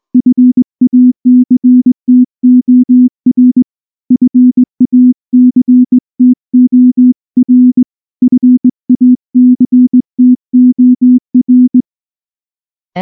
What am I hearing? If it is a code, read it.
Morse code, heard as a keyed tone, about 19 wpm, 250 Hz. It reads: FACTOR FACTOR FACTOR
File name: morse / morse-training-20WPM-5